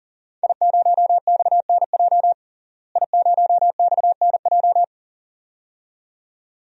Morse code, heard as a keyed tone, about 37 wpm, 700 Hz. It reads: I0XNJ I0XNJ
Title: DXCC-100 Call Signs - 5 Characters - Repeated 2x - 40wpm